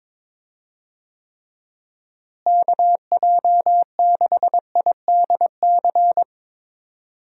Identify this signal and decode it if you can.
Morse code, heard as a keyed tone, about 22 wpm, 700 Hz. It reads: KJ6IDC
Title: USA Call Signs - 6 Characters - Repeated 2x - 22wpm